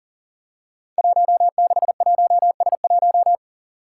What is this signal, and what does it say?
Morse code, heard as a keyed tone, about 38 wpm, 700 Hz. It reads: 161S1
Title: Five Challenging Letter-Number Combinations 40wpm